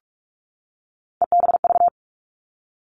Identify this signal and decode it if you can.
Morse code, heard as a keyed tone, about 44 wpm, 700 Hz. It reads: EBV
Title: All Cut Numbers - 3-Digit 45wpm